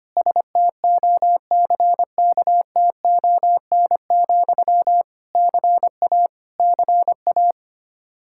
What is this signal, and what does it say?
Morse code, heard as a keyed tone, about 25 wpm, 700 Hz. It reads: STOCKTON, CA CA
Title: Practice QSO - Exchange 1 - Beginning - 25wpm